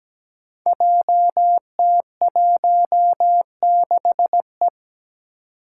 Morse code, heard as a keyed tone, about 17 wpm, 700 Hz. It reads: JT16E
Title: Five Challenging Letter-Number Combinations 17wpm